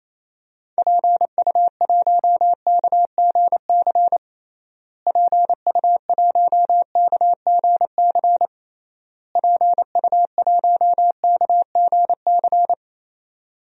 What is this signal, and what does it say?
Morse code, heard as a keyed tone, about 27 wpm, 700 Hz. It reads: PU1KGC PU1KGC PU1KGC
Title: DXCC-100 Call Signs - 6 Characters - Repeated 3x - 28wpm